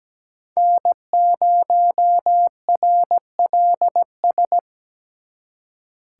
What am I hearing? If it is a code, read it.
Morse code, heard as a keyed tone, about 20 wpm, 700 Hz. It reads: N0RLS
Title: Speed Racing - USA Call Signs 17wpm